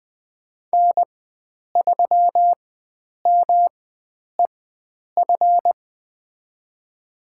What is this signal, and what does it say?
Morse code, heard as a keyed tone, about 20 wpm, spaced as for 9 wpm, 700 Hz. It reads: N3MEF
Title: USA Call Signs - 4x Character Spacing - 20wpm